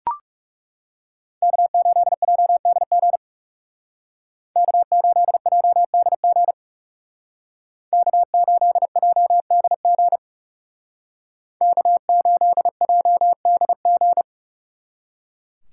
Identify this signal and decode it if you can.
Morse code, heard as a keyed tone, about 37 wpm, 700 Hz. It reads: K8JDG K8JDG K8JDG K8JDG
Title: Speed Racing - USA 1x3 Call Signs 30wpm